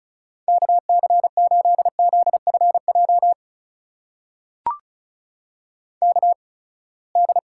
Morse code, heard as a keyed tone, about 36 wpm, spaced as for 24 wpm, 700 Hz. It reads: KC8ZFJ KD
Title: USA Call Signs - 6 Characters - 8x Character Spacing - 35wpm